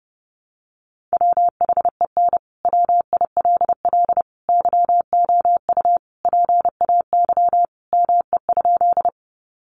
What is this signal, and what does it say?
Morse code, heard as a keyed tone, about 30 wpm, 700 Hz. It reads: WHEN WILL YOU PAY ME?
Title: Sentences from Top 700 Words 30wpm